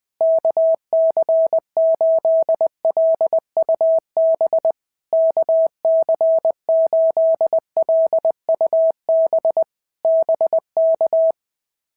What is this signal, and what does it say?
Morse code, heard as a keyed tone, about 21 wpm, 650 Hz. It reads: KC8LUB KC8LUB BK
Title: POTA QSO Practice - 20wpm